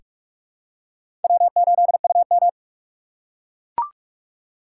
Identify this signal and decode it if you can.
Morse code, heard as a keyed tone, about 44 wpm, 700 Hz. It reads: W8UM
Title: USA Call Signs - 4 Characters - 4x Character Spacing - 45wpm